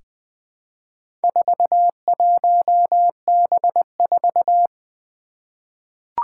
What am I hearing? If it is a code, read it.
Morse code, heard as a keyed tone, about 20 wpm, 700 Hz. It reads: 41B4
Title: Four Challenging Letter-Number Combinations 20wpm